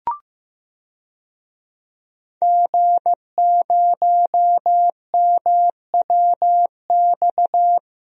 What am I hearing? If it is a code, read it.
Morse code, heard as a keyed tone, about 15 wpm, 700 Hz. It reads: G0MWX
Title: DXCC-100 Call Signs - 5 Characters - Repeated 2x - 15wpm